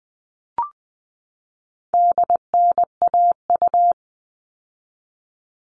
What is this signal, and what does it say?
Morse code, heard as a keyed tone, about 20 wpm, 700 Hz. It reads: DNAU
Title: All Cut Numbers - 4-Digit 20wpm